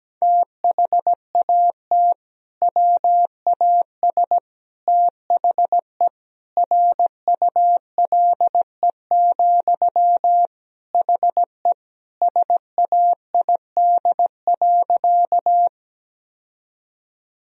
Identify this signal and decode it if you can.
Morse code, heard as a keyed tone, about 17 wpm, 700 Hz. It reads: THAT WAS THE RULE, HE SAID.
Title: Sentences from Top 1000 Words - Review Part 2 - 17wpm